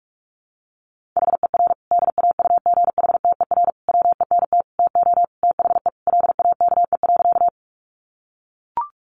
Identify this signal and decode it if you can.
Morse code, heard as a keyed tone, about 44 wpm, 700 Hz. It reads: HER DAUGHTER WENT TO THE LAKE.
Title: Sentences from Top 1500 Words 45wpm